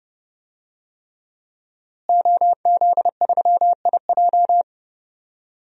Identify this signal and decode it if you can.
Morse code, heard as a keyed tone, about 31 wpm, 700 Hz. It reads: OZ3IJ
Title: DXCC-100 Call Signs - 5 Characters - Repeated 3x - 30wpm